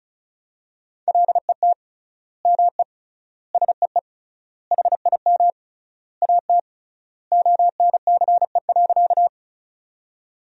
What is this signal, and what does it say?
Morse code, heard as a keyed tone, about 36 wpm, 700 Hz. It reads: LET ME SEE HIM AT ONCE.
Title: 3x Word-Spacing - Sentences from Top 300 Words - 35wpm